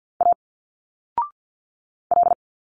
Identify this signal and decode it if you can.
Morse code, heard as a keyed tone, about 48 wpm, 700 Hz. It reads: A L
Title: Single Letters 50wpm